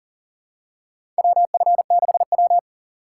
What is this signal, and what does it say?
Morse code, heard as a keyed tone, about 40 wpm, 700 Hz. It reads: WF6W
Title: USA Call Signs - 4 Characters - 4x Character Spacing - 40wpm